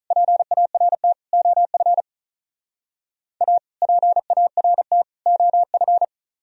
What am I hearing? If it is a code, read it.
Morse code, heard as a keyed tone, about 44 wpm, 700 Hz. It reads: PART OF A PART OF